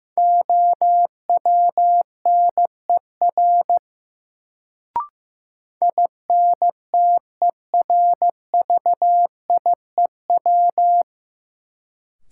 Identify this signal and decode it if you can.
Morse code, heard as a keyed tone, about 15 wpm, 700 Hz. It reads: OWNER INTERVIEW
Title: Sentences from Top 1600 Words - New Word Drill - 15wpm